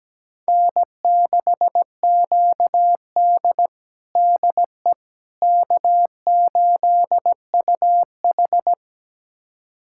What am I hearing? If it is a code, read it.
Morse code, heard as a keyed tone, about 17 wpm, 700 Hz. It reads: N6QD DE K8UH
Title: USA Callsign DE Callsign - 4 Characters - 17wpm